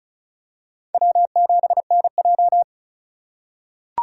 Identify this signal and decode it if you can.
Morse code, heard as a keyed tone, about 34 wpm, 700 Hz. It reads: W7NJ
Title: USA Call Signs - 2x Character Spacing - 35wpm